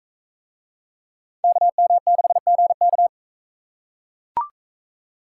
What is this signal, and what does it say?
Morse code, heard as a keyed tone, about 42 wpm, 700 Hz. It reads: KM6GK